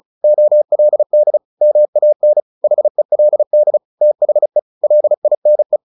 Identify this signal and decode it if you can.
Morse code, heard as a keyed tone, about 34 wpm, 600 Hz. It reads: OLD MAN HELD THE LINE
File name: the old man and the sea / the old man and the sea 1 35wpm